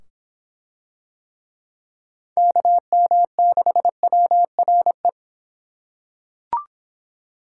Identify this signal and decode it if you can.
Morse code, heard as a keyed tone, about 24 wpm, 700 Hz. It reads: KM6WRE